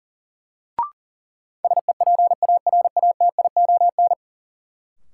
Morse code, heard as a keyed tone, about 40 wpm, 700 Hz. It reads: SEPARATION
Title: Sets of 5 Words - New Word Drill - 40wpm